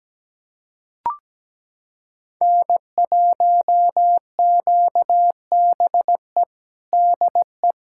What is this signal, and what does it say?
Morse code, heard as a keyed tone, about 17 wpm, 700 Hz. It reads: N1QBE DE
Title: USA Callsign DE Callsign - 5 Characters - 17wpm